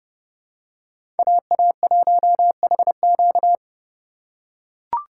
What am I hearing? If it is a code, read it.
Morse code, heard as a keyed tone, about 30 wpm, 700 Hz. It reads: AA1HQ
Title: USA Call Signs - 5 Characters - 4x Character Spacing - 30wpm